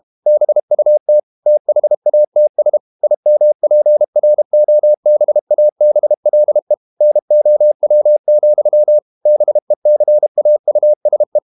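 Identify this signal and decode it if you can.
Morse code, heard as a keyed tone, about 32 wpm, 600 Hz. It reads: BUT THATS IMPROBABLE NOW, BECAUSE
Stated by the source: Morse code practice at 32 wpm from CW med Gnister, reading Ken Follett's novel Never in English.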